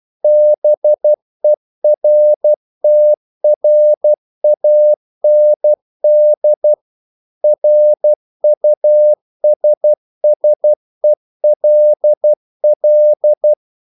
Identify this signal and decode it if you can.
Morse code, heard as a keyed tone, about 12 wpm, 600 Hz. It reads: BERTRAND RUSSELL
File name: CW / Practice / EnglishQuotations0112